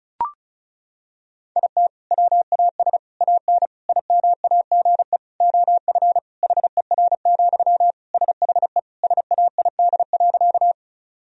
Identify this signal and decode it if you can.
Morse code, heard as a keyed tone, about 36 wpm, 700 Hz. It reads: IT WAS AN IMAGE OF HER, SHE SAID.
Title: Sentences from Top 1000 Words - Review Part 4 - 35wpm